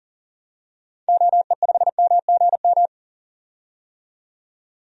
Morse code, heard as a keyed tone, about 40 wpm, 700 Hz. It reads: OE5MGM
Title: DXCC-100 Call Signs - 6 Characters - 40wpm